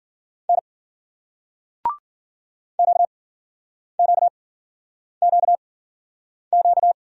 Morse code, heard as a keyed tone, about 55 wpm, spaced as for 9 wpm, 700 Hz. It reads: N QQQQ